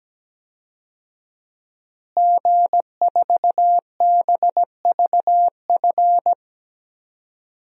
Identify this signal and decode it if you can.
Morse code, heard as a keyed tone, about 17 wpm, 700 Hz. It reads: G4BVF
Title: DXCC-100 Call Signs - 5 Characters - Repeated 3x - 17wpm